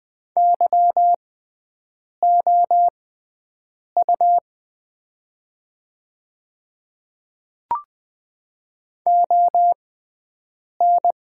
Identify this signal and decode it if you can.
Morse code, heard as a keyed tone, about 20 wpm, spaced as for 7 wpm, 700 Hz. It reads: YOU ON
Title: 6x Character Spacing - Top 100 Words - 20wpm